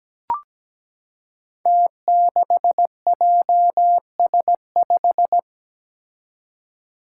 Morse code, heard as a keyed tone, about 17 wpm, 700 Hz. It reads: T6JS5
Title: Five Challenging Letter-Number Combinations 17wpm